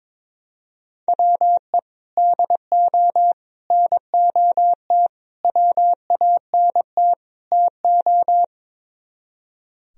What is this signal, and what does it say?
Morse code, heard as a keyed tone, about 22 wpm, 700 Hz. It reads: WE DO NOT WANT TO
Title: Speed-Racing - Sets of 5 Words 22wpm